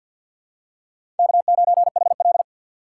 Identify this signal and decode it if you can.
Morse code, heard as a keyed tone, about 46 wpm, 700 Hz. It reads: K9HL
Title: USA Call Signs - 2x Character Spacing - 50wpm